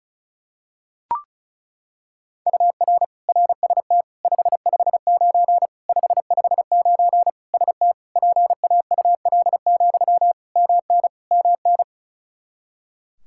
Morse code, heard as a keyed tone, about 36 wpm, 700 Hz. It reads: UR RST 559 559 ST PAUL, MN MN
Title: Practice QSO - Exchange 1 - Beginning - 35wpm